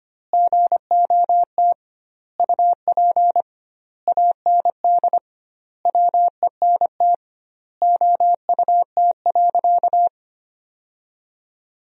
Morse code, heard as a keyed tone, about 25 wpm, 700 Hz. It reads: GOT UP AND WENT OUT.
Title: 2x Word-Spacing - Sentences from Top 300 Words - 25wpm